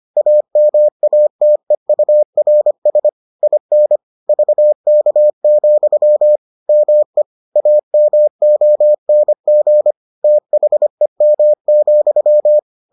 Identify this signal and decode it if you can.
Morse code, heard as a keyed tone, about 25 wpm, 600 Hz. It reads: AMATEURS IN VK, ME AMONG THEM,